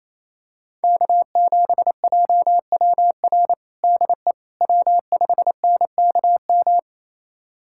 Morse code, heard as a keyed tone, about 28 wpm, 700 Hz. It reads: K7JWR DE W5NKM
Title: USA Callsign DE Callsign 28wpm